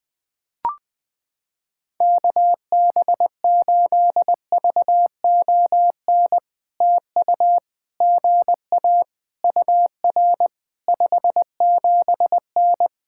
Morse code, heard as a keyed tone, about 20 wpm, 700 Hz. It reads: KB8VON TU GA UR 57N
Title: POTA QSO Practice - 20wpm